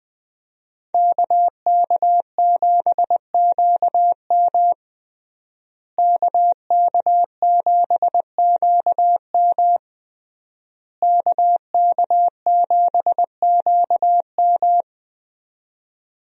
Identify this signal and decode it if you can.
Morse code, heard as a keyed tone, about 20 wpm, 700 Hz. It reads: KK7QM KK7QM KK7QM
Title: USA Call Signs - 5 Characters - Repeated 3x - 20wpm